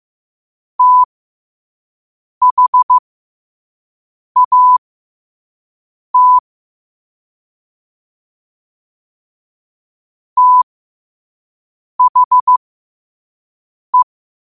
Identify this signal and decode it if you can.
Morse code, heard as a keyed tone, about 15 wpm, spaced as for 5 wpm, 1000 Hz. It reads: THAT THE
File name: qotdfiles / 5wpm